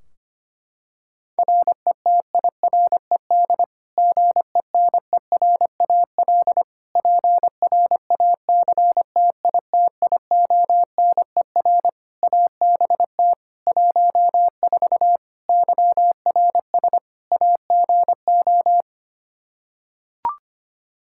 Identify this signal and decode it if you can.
Morse code, heard as a keyed tone, about 25 wpm, 700 Hz. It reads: RETIRED GENERAL PRACTITIONER ABT 14 YRS AGO
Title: Practice QSO - Exchange 3 - More Info - 25wpm